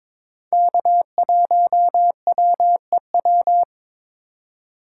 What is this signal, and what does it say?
Morse code, heard as a keyed tone, about 22 wpm, 700 Hz. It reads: K1WEW
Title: USA Call Signs - 4x Character Spacing - 22wpm